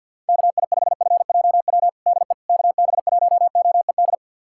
Morse code, heard as a keyed tone, about 51 wpm, 700 Hz. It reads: KI5FJW DE KB1OED
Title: USA Callsign DE Callsign - 6 Characters - 50wpm